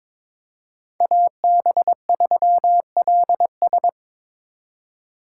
Morse code, heard as a keyed tone, about 25 wpm, 700 Hz. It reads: AB3LS
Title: Speed Racing - USA 2x2 Call Signs 22wpm